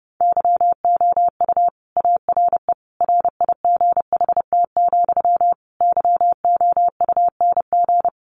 Morse code, heard as a keyed tone, about 30 wpm, 700 Hz. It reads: YOU ARE RIGHT, YOUNG
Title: Sentences from Top 400 Words 30wpm